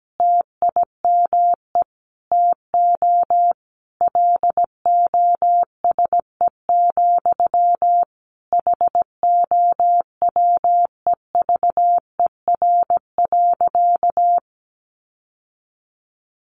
Morse code, heard as a keyed tone, about 17 wpm, 700 Hz. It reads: TIME TO LOSE, HOWEVER.
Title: Sentences from Top 1100 Words 17wpm